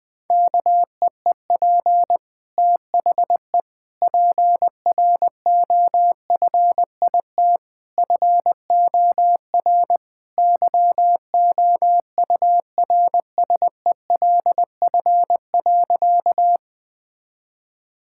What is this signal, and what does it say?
Morse code, heard as a keyed tone, about 20 wpm, 700 Hz. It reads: KEEP THE PROFIT FOR YOURSELF.
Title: Sentences from Top 1000 Words - Review Part 5 - 20wpm